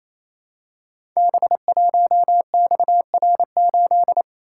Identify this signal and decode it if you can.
Morse code, heard as a keyed tone, about 27 wpm, 700 Hz. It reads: B1XR8
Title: Code Groups - 5 Letter:Numbers - 28wpm